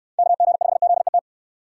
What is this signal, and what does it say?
Morse code, heard as a keyed tone, about 69 wpm, 700 Hz. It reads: KG5ZET